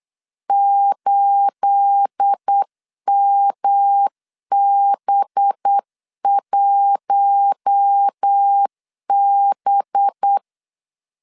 Morse code, heard as a keyed tone, about 8 wpm, 800 Hz. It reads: 8MB1B
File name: CW / DET / 1 9 1500C S A 8wpm 5637